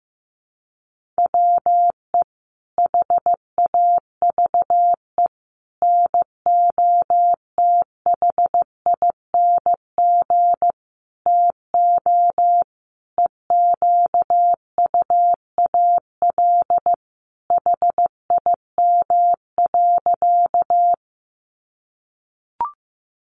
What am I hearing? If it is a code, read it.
Morse code, heard as a keyed tone, about 15 wpm, 700 Hz. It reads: WE HAVE NOTHING TO EQUAL HIM.
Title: Sentences from Top 1400 Words 15wpm